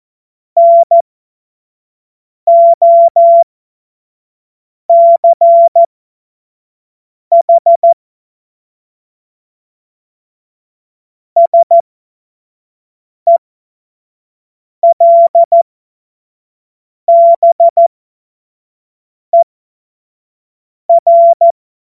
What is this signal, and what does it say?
Morse code, heard as a keyed tone, about 14 wpm, spaced as for 5 wpm, 680 Hz. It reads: NOCH SELBER
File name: Morsekurs / Witze / Witz 06 Februar 2024 25 F70